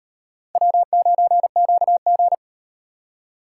Morse code, heard as a keyed tone, about 36 wpm, 700 Hz. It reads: W9QG